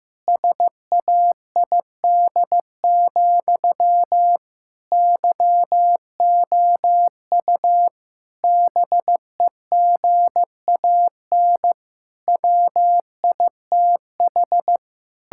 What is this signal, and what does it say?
Morse code, heard as a keyed tone, about 16 wpm, 700 Hz. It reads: SAID, YOU BEGAN WITH
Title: Sentences from Top 1600 Words 15wpm